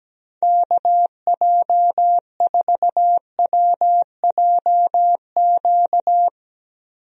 Morse code, heard as a keyed tone, about 17 wpm, 700 Hz. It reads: KJ4WJQ
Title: USA Call Signs - 6 Characters - Repeated 2x - 17wpm